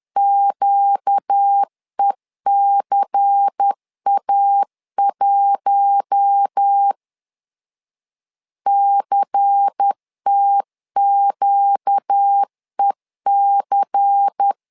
Morse code, heard as a keyed tone, about 10 wpm, 800 Hz. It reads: QECA1 CTQEC
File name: LEZIONI / 5 LBVC / 5 LBVC 10wpm 2349